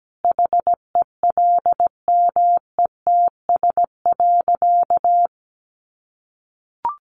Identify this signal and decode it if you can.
Morse code, heard as a keyed tone, about 17 wpm, 700 Hz. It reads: HELMETS.
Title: A Sound Of Thunder 17wpm